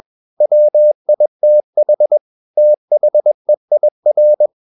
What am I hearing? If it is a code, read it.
Morse code, heard as a keyed tone, about 21 wpm, 600 Hz. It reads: WITH THEIR
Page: CW med Gnister